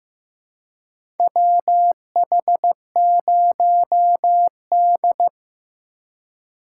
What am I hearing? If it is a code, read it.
Morse code, heard as a keyed tone, about 16 wpm, 700 Hz. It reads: WH0D
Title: USA Call Signs - 4 Characters - 2x Character Spacing - 15wpm